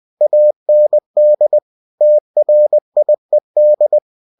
Morse code, heard as a keyed tone, about 20 wpm, 600 Hz. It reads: AND TRIED
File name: cw-books / winnie-the-pooh / winnie-20wpm-page-0045